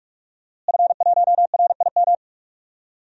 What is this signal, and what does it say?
Morse code, heard as a keyed tone, about 41 wpm, 700 Hz. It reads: F1RIM
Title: DXCC-100 Call Signs - 5 Characters - Repeated 2x - 45wpm